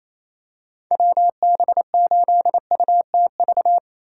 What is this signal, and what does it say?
Morse code, heard as a keyed tone, about 28 wpm, 700 Hz. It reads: WB8UTV
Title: USA Call Signs - 4x Character Spacing - 28wpm